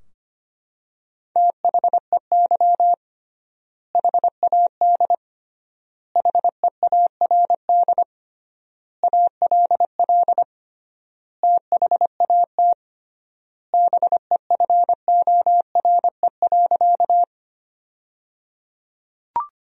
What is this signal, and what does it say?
Morse code, heard as a keyed tone, about 25 wpm, 700 Hz. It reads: THEY HAD HEARD ALL THAT BEFORE.
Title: 3x Word-Spacing - Sentences from Top 400 Words - 25wpm